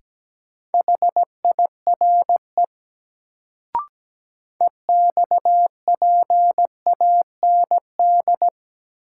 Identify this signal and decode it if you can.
Morse code, heard as a keyed tone, about 17 wpm, 700 Hz. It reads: HIRE EXPAND
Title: Sentences from Top 1400 Words - New Word Drill Alt - 17wpm